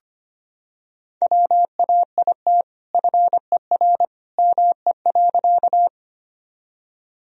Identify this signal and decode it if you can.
Morse code, heard as a keyed tone, about 25 wpm, 700 Hz. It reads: WAIT FER ME.
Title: Sentences from Top 500 Words and Abbreviations 25wpm